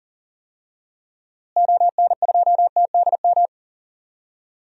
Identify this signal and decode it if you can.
Morse code, heard as a keyed tone, about 40 wpm, 700 Hz. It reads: ON2TDM
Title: DXCC-100 Call Signs - 6 Characters - Repeated 2x - 40wpm